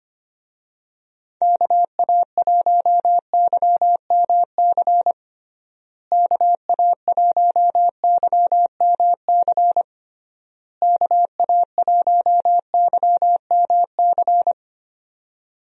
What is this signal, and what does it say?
Morse code, heard as a keyed tone, about 25 wpm, 700 Hz. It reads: KA1YMC KA1YMC KA1YMC
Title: USA Call Signs - 6 Characters - Repeated 3x - 25wpm